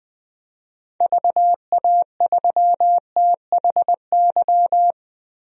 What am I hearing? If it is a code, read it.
Morse code, heard as a keyed tone, about 21 wpm, 700 Hz. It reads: VA3THY
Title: DXCC-100 Call Signs - 6 Characters - Repeated 3x - 20wpm